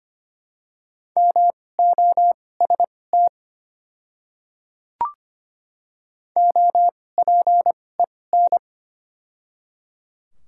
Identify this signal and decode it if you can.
Morse code, heard as a keyed tone, about 25 wpm, spaced as for 18 wpm, 690 Hz. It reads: MOST OPEN
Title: 2x Character Spacing - Top 300 Words - 25wpm